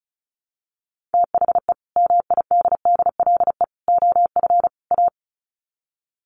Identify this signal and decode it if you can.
Morse code, heard as a keyed tone, about 34 wpm, 700 Hz. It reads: THE MIDDLE OF A
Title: Sets of 4 Words 35wpm